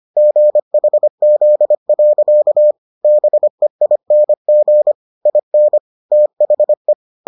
Morse code, heard as a keyed tone, about 25 wpm, 600 Hz. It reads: GHZ. BEING IN THE